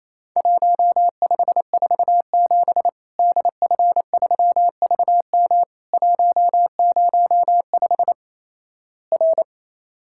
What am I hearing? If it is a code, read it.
Morse code, heard as a keyed tone, about 28 wpm, 700 Hz. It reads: 1547 DF3VM 105 R
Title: Worked All Europe DX Contest - QTC Exchange - 28wpm